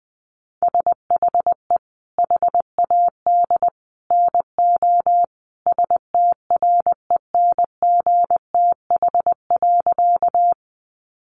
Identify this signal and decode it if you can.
Morse code, heard as a keyed tone, about 20 wpm, 700 Hz. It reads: SHE HAD NO STRENGTH.
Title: Sentences from Top 1300 Words 20wpm